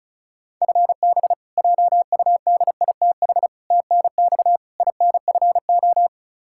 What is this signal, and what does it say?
Morse code, heard as a keyed tone, about 36 wpm, 700 Hz. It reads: FB JUDITH TNX INFO
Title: K1USN SST QSO Practice 35wpm